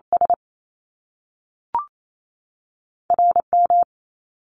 Morse code, heard as a keyed tone, about 28 wpm, 700 Hz. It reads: S RM
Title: Top Two Letter Combinations in Words 28wpm